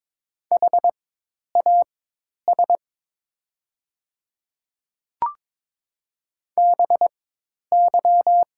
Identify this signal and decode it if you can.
Morse code, heard as a keyed tone, about 22 wpm, spaced as for 10 wpm, 700 Hz. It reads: HAS BY
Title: 4x Character Spacing - Top 100 Words - 22wpm